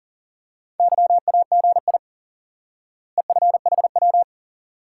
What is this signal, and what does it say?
Morse code, heard as a keyed tone, about 40 wpm, 700 Hz. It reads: YAGI EFHW
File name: IQE Course 3 - Lesson 89 - YAGI - Two Elements - 40wpm